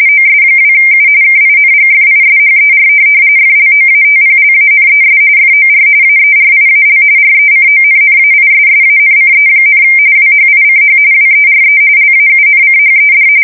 RTTY, about 45.45 baud. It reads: OS2U KINGFISHER FLOATPLANES FOR AERIAL RECONNAISSANCE, WHICH WERE LAUNCHED
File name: fsk170 / USS-Massachusetts